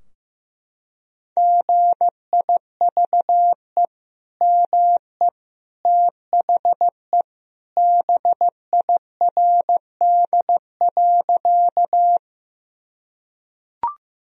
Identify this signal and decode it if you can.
Morse code, heard as a keyed tone, about 15 wpm, 700 Hz. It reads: GIVE ME THE BIRD.
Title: Sentences from Top 1000 Words - Review Part 2 - 15wpm